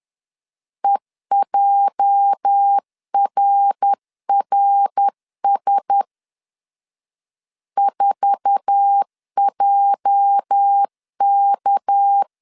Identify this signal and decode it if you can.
Morse code, heard as a keyed tone, about 10 wpm, 800 Hz. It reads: EJRRS 4JK